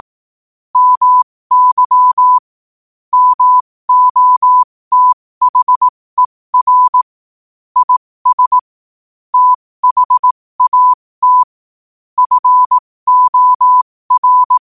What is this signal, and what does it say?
Morse code, heard as a keyed tone, about 18 wpm, spaced as for 15 wpm, 1000 Hz. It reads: MY MOTHER IS THAT FOR
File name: qotdfiles / 15wpm